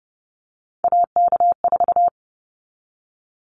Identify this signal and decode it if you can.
Morse code, heard as a keyed tone, about 30 wpm, 700 Hz. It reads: AK4
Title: Callsign Prefixes for Top 25 DXCC - 30wpm